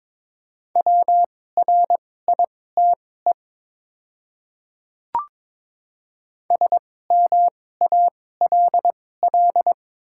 Morse code, heard as a keyed tone, about 22 wpm, spaced as for 16 wpm, 700 Hz. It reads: WRITE SMALL
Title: 2x Character Spacing - Top 200 Words - 22wpm